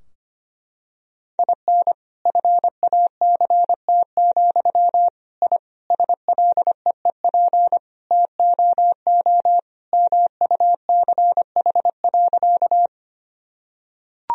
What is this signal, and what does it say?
Morse code, heard as a keyed tone, about 25 wpm, 700 Hz. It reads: IN FACT, I SLEEP TOO MUCH.
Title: Sentences from Top 1000 Words - Review Part 3 - 25wpm